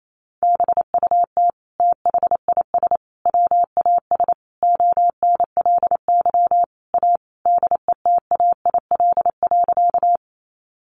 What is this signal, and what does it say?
Morse code, heard as a keyed tone, about 28 wpm, 700 Hz. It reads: BUT THIS WAS ONLY A DETAIL.
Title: Sentences from Top 1500 Words 28wpm